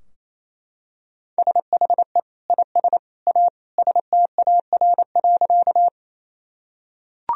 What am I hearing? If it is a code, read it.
Morse code, heard as a keyed tone, about 28 wpm, 700 Hz. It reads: SHE IS A STAR.
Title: Sentences from Top 1000 Words - Review Part 2 - 28wpm